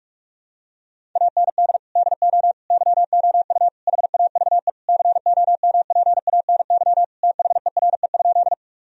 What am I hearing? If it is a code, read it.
Morse code, heard as a keyed tone, about 46 wpm, 700 Hz. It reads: AND DO YOU HAVE COMPANY THERE?
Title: Sentences from Top 1000 Words - Review Part 3 - 45wpm